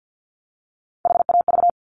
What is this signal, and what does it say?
Morse code, heard as a keyed tone, about 50 wpm, 700 Hz. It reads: HAV
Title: Top Three Letter Combinations in Words 50wpm